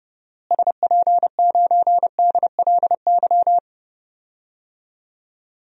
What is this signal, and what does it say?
Morse code, heard as a keyed tone, about 30 wpm, 700 Hz. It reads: SP9DLY
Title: DXCC-100 Call Signs - 6 Characters - Repeated 3x - 30wpm